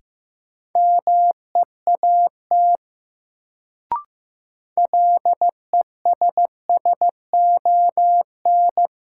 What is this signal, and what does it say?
Morse code, heard as a keyed tone, about 15 wpm, 700 Hz. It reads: MEAT LESSON
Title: Sentences from Top 1200 Words - New Word Drill Alt - 15wpm